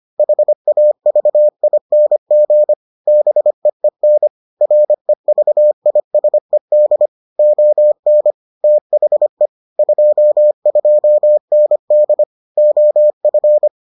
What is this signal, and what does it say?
Morse code, heard as a keyed tone, about 25 wpm, 600 Hz. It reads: HAVING BEEN REVISED ON THE 22ND OF